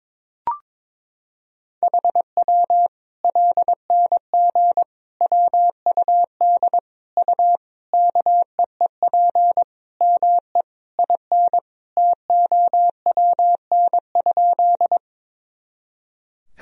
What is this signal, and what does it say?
Morse code, heard as a keyed tone, about 22 wpm, 700 Hz. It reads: HW LNG WUD U KEEP ME IN TOWN?
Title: Sentences from Top 500 Words and Abbreviations 22wpm